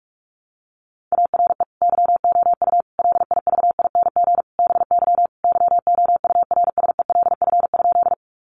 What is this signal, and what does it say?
Morse code, heard as a keyed tone, about 44 wpm, 700 Hz. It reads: ARE YOU LIVING BY YOURSELF?
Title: Sentences from Top 1100 Words 45wpm